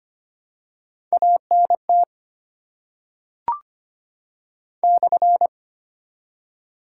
Morse code, heard as a keyed tone, about 25 wpm, 690 Hz. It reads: ANT /
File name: Top 90 QSO Elements - 8x Character Spacing - 25wpm